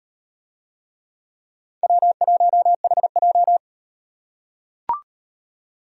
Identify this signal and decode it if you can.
Morse code, heard as a keyed tone, about 35 wpm, 700 Hz. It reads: W1HJ